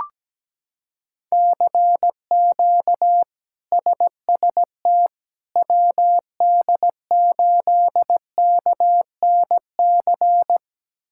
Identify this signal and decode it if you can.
Morse code, heard as a keyed tone, about 17 wpm, 700 Hz. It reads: CQ SST WD8KNC
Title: K1USN SST QSO Practice - 17wpm